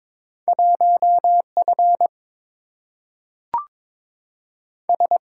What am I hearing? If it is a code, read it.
Morse code, heard as a keyed tone, about 22 wpm, 700 Hz. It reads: 1 F S
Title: Single-Letter Single-Number 22wpm